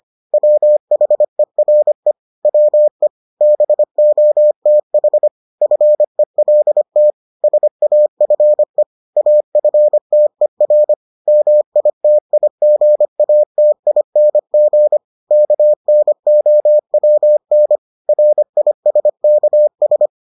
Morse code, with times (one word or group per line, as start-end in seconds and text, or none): 0.3-2.1 WHERE
2.5-3.1 WE
3.4-5.3 BOTH
5.6-7.1 FELT
7.4-8.8 SAFE
9.2-10.9 AFTER
11.3-15.0 MITIGATING
15.3-17.8 KNOWN
18.1-20.1 RISKS